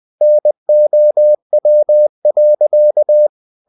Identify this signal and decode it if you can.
Morse code, heard as a keyed tone, about 20 wpm, 600 Hz. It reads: NOW.
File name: cw-books / winnie-the-pooh / winnie-20wpm-page-0053